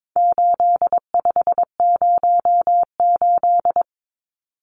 Morse code, heard as a keyed tone, about 22 wpm, 700 Hz. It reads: 8508
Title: Numbers - 4-Digit 22wpm